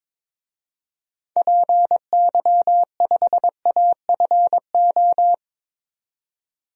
Morse code, heard as a keyed tone, about 22 wpm, 700 Hz. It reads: PY5AFO
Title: DXCC-100 Call Signs - 6 Characters - 22wpm